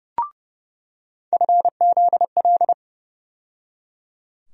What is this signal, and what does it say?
Morse code, heard as a keyed tone, about 30 wpm, 700 Hz. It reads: FZL
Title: USA Call Sign Postfixes - 3 Letters 30wpm